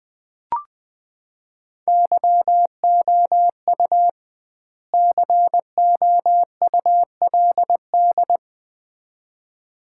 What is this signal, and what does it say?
Morse code, heard as a keyed tone, about 20 wpm, 700 Hz. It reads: YOU COULD
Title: 2x Word-Spacing - Sets of 2 Words - 20wpm